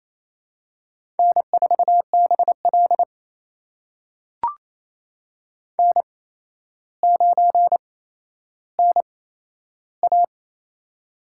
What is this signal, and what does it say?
Morse code, heard as a keyed tone, about 29 wpm, spaced as for 12 wpm, 700 Hz. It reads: N4BL N9NA
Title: USA Call Signs - 4 Characters - 8x Character Spacing - 28wpm